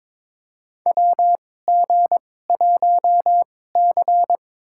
Morse code, heard as a keyed tone, about 22 wpm, spaced as for 16 wpm, 700 Hz. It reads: WG1C
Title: USA Call Signs - 4 Characters - 2x Character Spacing - 22wpm